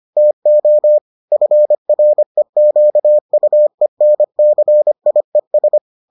Morse code, heard as a keyed tone, about 25 wpm, 600 Hz. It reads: TO FREQUENCIES